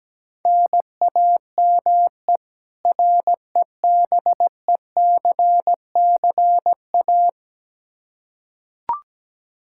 Morse code, heard as a keyed tone, about 17 wpm, 700 Hz. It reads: NAME REBECCA
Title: K1USN SST QSO Practice 17wpm